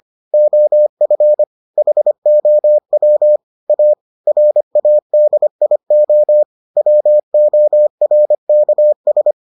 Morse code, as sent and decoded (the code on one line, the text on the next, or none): --- ..-. / .... --- .-- / .- / .-. .- -.. .. --- / .-- --- .-. -.- ...
OF HOW A RADIO WORKS